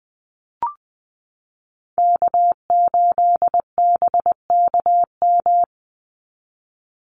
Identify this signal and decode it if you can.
Morse code, heard as a keyed tone, about 20 wpm, 700 Hz. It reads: K8BKM
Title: Call Signs Repeated 2x - 20wpm